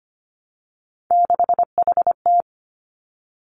Morse code, heard as a keyed tone, about 25 wpm, 700 Hz. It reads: 6HT
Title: Three Challenging Letter-Number Combinations 25wpm